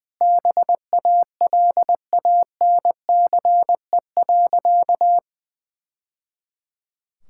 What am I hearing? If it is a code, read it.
Morse code, heard as a keyed tone, about 20 wpm, 700 Hz. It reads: BALANCE.
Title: Sentences from Top 1000 Words - Review Part 6 - 20wpm